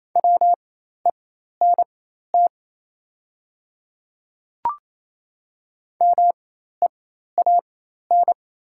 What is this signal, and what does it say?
Morse code, heard as a keyed tone, about 28 wpm, spaced as for 13 wpm, 700 Hz. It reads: WENT MEAN
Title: 4x Character Spacing - Top 200 Words - 28wpm